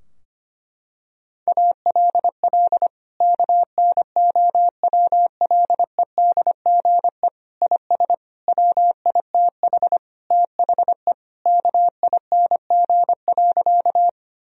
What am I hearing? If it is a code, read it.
Morse code, heard as a keyed tone, about 25 wpm, 700 Hz. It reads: ALL KNOWLEDGE IS WITH THE KING.
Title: Sentences from Top 1000 Words - Review Part 3 - 25wpm